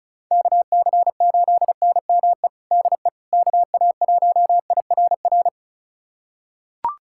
Morse code, heard as a keyed tone, about 34 wpm, 700 Hz. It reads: KC8NME DE KA1IRR
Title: USA Callsign DE Callsign - 6 Characters - 35wpm